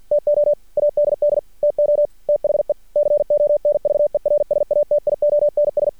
Morse code, heard as a keyed tone, about 50 wpm, 590 Hz. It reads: TO ADD TO THE CONVERSATIONS
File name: archived / CRSnet